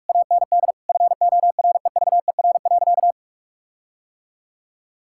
Morse code, heard as a keyed tone, about 41 wpm, 700 Hz. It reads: AND FOREVER.
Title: Sentences from Top 1600 Words 45wpm